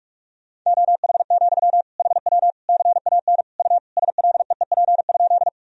Morse code, heard as a keyed tone, about 46 wpm, 700 Hz. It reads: OH, HW CAN U SLEEP?
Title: Sentences from Top 500 Words and Abbreviations 45wpm